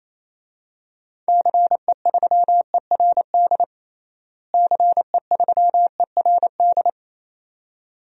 Morse code, heard as a keyed tone, about 27 wpm, 700 Hz. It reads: CE3ERD CE3ERD
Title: DXCC-100 Call Signs - 6 Characters - Repeated 2x - 28wpm